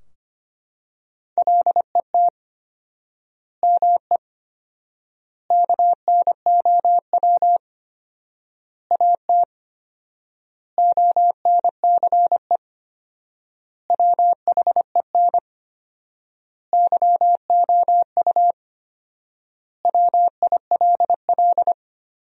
Morse code, heard as a keyed tone, about 25 wpm, 700 Hz. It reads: LET ME KNOW AT ONCE WHEN YOU WILL
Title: 4x Word-Spacing - Sentences from Top 300 Words - 25wpm